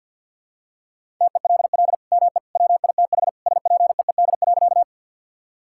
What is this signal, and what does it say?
Morse code, heard as a keyed tone, about 46 wpm, 700 Hz. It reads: TELL ME WITH SPEED.
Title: Sentences from Top 1000 Words - Review Part 5 - 50wpm